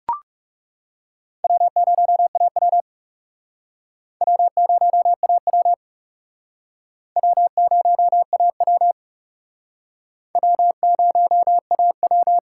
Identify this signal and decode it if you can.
Morse code, heard as a keyed tone, about 40 wpm, 700 Hz. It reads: W0AW W0AW W0AW W0AW